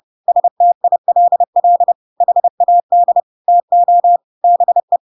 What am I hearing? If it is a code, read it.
Morse code, heard as a keyed tone, about 30 wpm, 700 Hz. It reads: STILL HAD TO BE
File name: Pinocchio / Pinocchio ch3 30 30 ik1hge